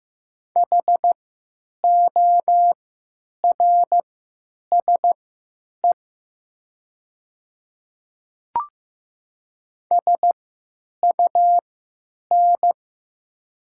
Morse code, heard as a keyed tone, about 15 wpm, spaced as for 9 wpm, 700 Hz. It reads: HORSE SUN
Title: 3x Character Spacing - Top 300 Words - 15wpm